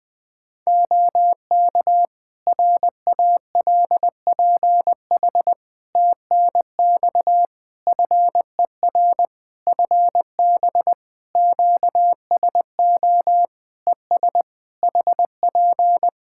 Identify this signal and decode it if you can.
Morse code, heard as a keyed tone, about 21 wpm, 700 Hz. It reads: OK RALPH TNX FER FB QSO ES HP
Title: Practice QSO - Exchange 4 - Ending - 20wpm